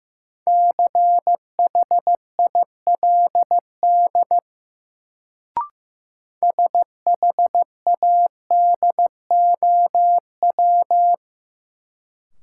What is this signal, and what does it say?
Morse code, heard as a keyed tone, about 15 wpm, 700 Hz. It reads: CHILD SHADOW
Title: Sets of 5 Words Encore - New Word Drill - 15wpm